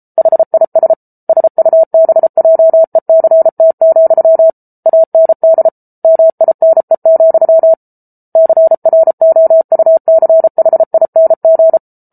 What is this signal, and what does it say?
Morse code, heard as a keyed tone, about 34 wpm, 650 Hz. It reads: HIS SUBJECT, AND MINE, CROUCHING